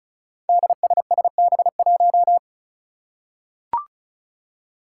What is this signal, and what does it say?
Morse code, heard as a keyed tone, about 36 wpm, 700 Hz. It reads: DSSB1
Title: Five Challenging Letter-Number Combinations 35wpm